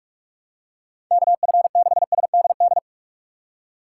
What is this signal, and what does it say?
Morse code, heard as a keyed tone, about 41 wpm, 700 Hz. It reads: KF6SDD
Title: USA Call Signs - 6 Characters - 2x Character Spacing - 45wpm